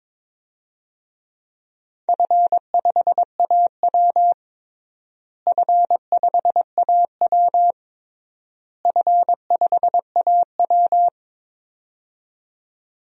Morse code, heard as a keyed tone, about 22 wpm, 700 Hz. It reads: F5AW F5AW F5AW
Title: DXCC-100 Call Signs - 4 Characters - Repeated 3x - 22wpm